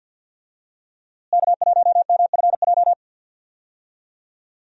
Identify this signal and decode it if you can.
Morse code, heard as a keyed tone, about 51 wpm, 700 Hz. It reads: K1MFJ
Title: USA Call Signs - 5 Characters - Repeated 3x - 50wpm